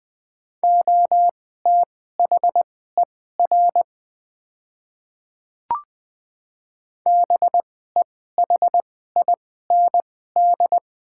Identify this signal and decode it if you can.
Morse code, heard as a keyed tone, about 20 wpm, spaced as for 14 wpm, 690 Hz. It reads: OTHER BEHIND
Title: 2x Character Spacing - Top 500 Words - 20wpm